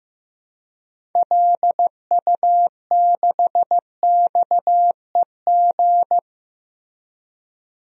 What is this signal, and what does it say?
Morse code, heard as a keyed tone, about 15 wpm, 700 Hz. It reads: LU6XEG
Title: DXCC-100 Call Signs - 6 Characters - 15wpm